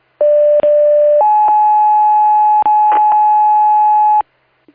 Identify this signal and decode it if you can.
Two-tone page, heard as a one-second tone, then a three-second tone, about 590 then 850 Hz, with noise. It is heard from 0.2 to 4.2 s.